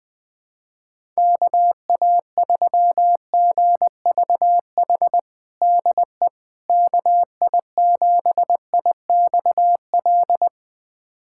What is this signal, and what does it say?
Morse code, heard as a keyed tone, about 20 wpm, 700 Hz. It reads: KA3GVH DE KI7IXL
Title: USA Callsign DE Callsign - 6 Characters - 20wpm